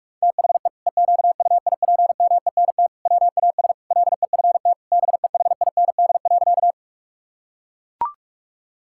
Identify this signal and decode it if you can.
Morse code, heard as a keyed tone, about 41 wpm, 700 Hz. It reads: THE EQUIPMENT WAS LEFT BEHIND.
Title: Sentences from Top 1000 Words - Review Part 5 - 45wpm